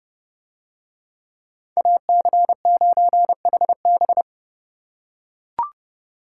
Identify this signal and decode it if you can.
Morse code, heard as a keyed tone, about 31 wpm, 700 Hz. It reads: AC9HB